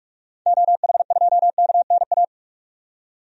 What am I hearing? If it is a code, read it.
Morse code, heard as a keyed tone, about 42 wpm, 700 Hz. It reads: OH2KNA
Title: DXCC-100 Call Signs - 6 Characters - 45wpm